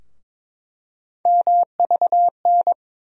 Morse code, heard as a keyed tone, about 21 wpm, 700 Hz. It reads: MVN
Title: USA Call Sign Postfixes - 3 Letters 22wpm